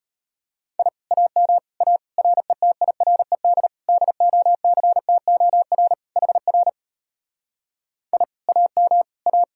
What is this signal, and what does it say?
Morse code, heard as a keyed tone, about 33 wpm, 700 Hz. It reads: I AM A RETIRED DOCTOR HR I AM A